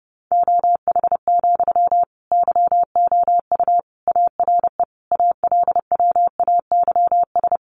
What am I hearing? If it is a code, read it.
Morse code, heard as a keyed tone, about 30 wpm, 700 Hz. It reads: OH, YOU ARE ALWAYS
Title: Sentences from Top 900 Words 30wpm